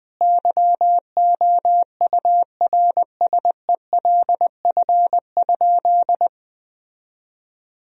Morse code, heard as a keyed tone, about 20 wpm, 700 Hz. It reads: YOURSELF?
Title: Sentences from Top 1000 Words - Review Part 4 - 20wpm